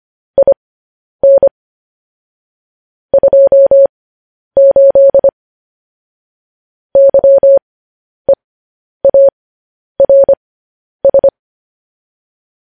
Morse code, heard as a keyed tone, about 25 wpm, spaced as for 10 wpm, 550 Hz. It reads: IN 28 YEARS